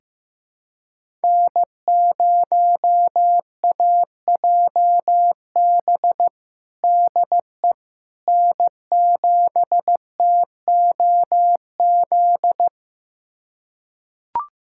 Morse code, heard as a keyed tone, about 15 wpm, 700 Hz. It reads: N0AJB DE N7TOZ
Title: USA Callsign DE Callsign - 5 Characters - 15wpm